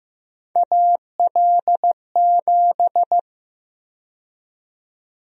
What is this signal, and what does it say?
Morse code, heard as a keyed tone, about 15 wpm, 700 Hz. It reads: AL7
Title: USA Call Sign Prefixes - 15wpm